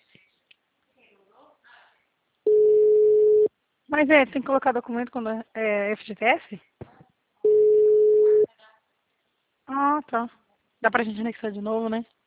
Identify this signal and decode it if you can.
Ringback tone (Europe, 425 Hz), with speech.
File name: out- -2012- - - .28